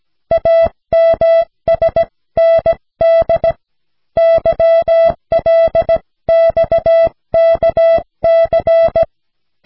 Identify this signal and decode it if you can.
Morse code, heard as a keyed tone, about 17 wpm, 660 Hz. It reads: AMSND YLXKC
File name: cw / snelheid 4